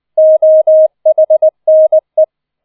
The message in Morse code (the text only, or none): OHNE